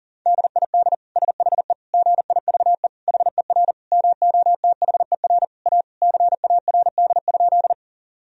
Morse code, heard as a keyed tone, about 37 wpm, 700 Hz. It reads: DID SHE GIVE HER MOTHER A CARD?
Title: Sentences from Top 1000 Words - Review Part 4 - 40wpm